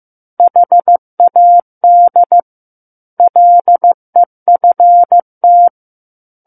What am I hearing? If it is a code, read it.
Morse code, heard as a keyed tone, about 15 wpm, 700 Hz. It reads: HAD LEFT
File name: Book One / 06-Chapter 6 - 15 WPM